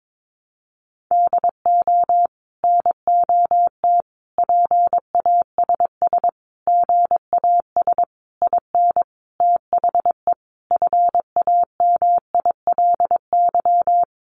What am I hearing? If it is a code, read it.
Morse code, heard as a keyed tone, about 22 wpm, 700 Hz. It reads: DO NOT PASS GAS IN THE FAMILY
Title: Sentences from Top 1100 Words 22wpm